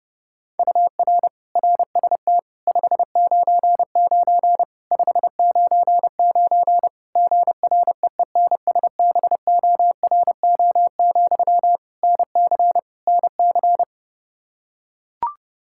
Morse code, heard as a keyed tone, about 30 wpm, 700 Hz. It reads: UR RST 599 599 GREENSBORO, NC NC
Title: Practice QSO - Exchange 1 - Beginning - 30wpm